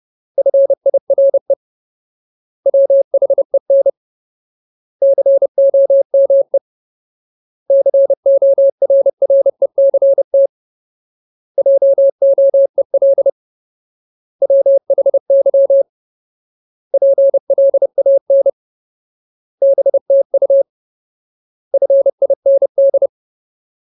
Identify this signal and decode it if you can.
Morse code, heard as a keyed tone, about 30 wpm, 550 Hz. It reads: FIRE WHEN COME CORRECT JOEL WHY PLAN BTU FIND
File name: Ultimate Word List / 30WPM Ultimate